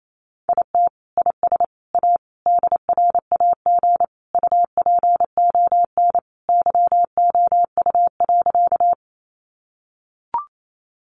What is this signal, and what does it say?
Morse code, heard as a keyed tone, about 28 wpm, 700 Hz. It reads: IT IS A DRAG UPON YOU.
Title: Sentences from Top 1500 Words 28wpm